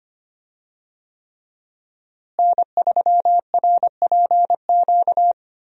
Morse code, heard as a keyed tone, about 25 wpm, 700 Hz. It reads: N3RPQ
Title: USA Call Signs - 5 Characters - Repeated 3x - 25wpm